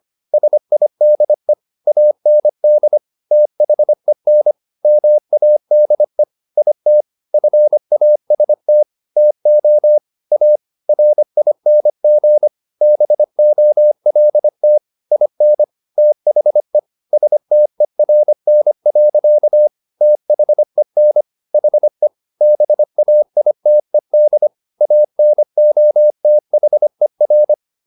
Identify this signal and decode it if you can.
Morse code, heard as a keyed tone, about 25 wpm, 600 Hz. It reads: SIDE AND THEN MADE IT FAST TO A RING BOLT IN THE STERN. THEN HE BAITED ANOTHER
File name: the old man and the sea part 1 25wpm